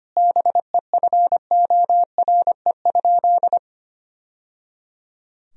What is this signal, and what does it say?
Morse code, heard as a keyed tone, about 25 wpm, 700 Hz. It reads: BEFORE?
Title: Sentences from Top 1000 Words - Review Part 3 - 25wpm